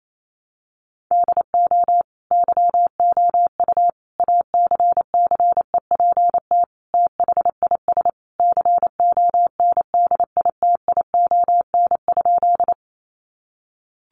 Morse code, heard as a keyed tone, about 28 wpm, 700 Hz. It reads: DO YOU ACCEPT THIS CONDITION?
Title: Sentences from Top 1300 Words 28wpm